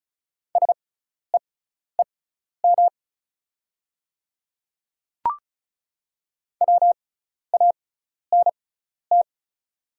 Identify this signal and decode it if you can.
Morse code, heard as a keyed tone, about 34 wpm, spaced as for 12 wpm, 700 Hz. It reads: SEEM WANT
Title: 6x Character Spacing - Top 300 Words - 35wpm